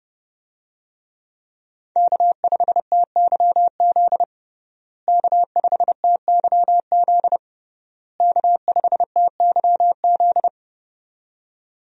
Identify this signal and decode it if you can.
Morse code, heard as a keyed tone, about 30 wpm, 700 Hz. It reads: K5TYZ K5TYZ K5TYZ
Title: USA Call Signs - 5 Characters - Repeated 3x - 30wpm